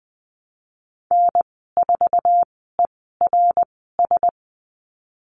Morse code, heard as a keyed tone, about 19 wpm, spaced as for 14 wpm, 700 Hz. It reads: N4ERS
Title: 069 Lesson 12g - Letter d - Call Signs 20wpm